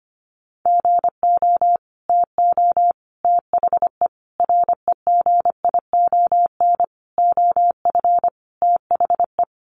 Morse code, heard as a keyed tone, about 25 wpm, 700 Hz. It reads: GO TO THE REGION OF THE
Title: Sentences from Top 1500 Words 25wpm